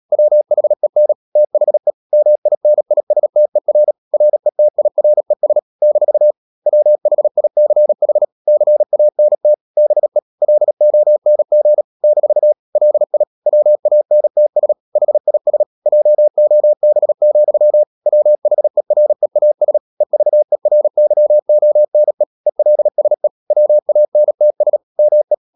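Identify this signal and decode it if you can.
Morse code, heard as a keyed tone, about 37 wpm, 600 Hz. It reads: WHEN THE MINISTER RETIRES - WHICH CANT BE LONG - LI WANTS HIS JOB, WHEREAS EVERYONE ELSE WANTS ME
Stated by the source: Morse code practice at 35–39 wpm sending English text of Ken Follett's novel Never, from CW med Gnister.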